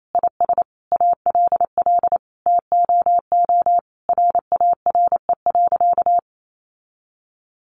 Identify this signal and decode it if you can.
Morse code, heard as a keyed tone, about 28 wpm, 700 Hz. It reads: IS ALL TOO RARE.
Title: Sentences from Top 1500 Words 28wpm